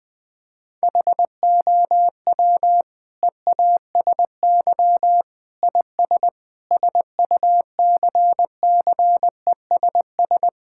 Morse code, heard as a keyed tone, about 20 wpm, 700 Hz. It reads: HOW EASY IS SUCCESS
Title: Sentences from Top 1000 Words - Review Part 6 - 20wpm